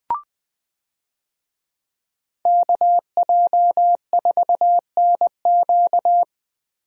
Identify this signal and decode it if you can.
Morse code, heard as a keyed tone, about 19 wpm, 700 Hz. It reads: KJ4NQ
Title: USA Call Signs - 5 Characters - Repeated 3x - 20wpm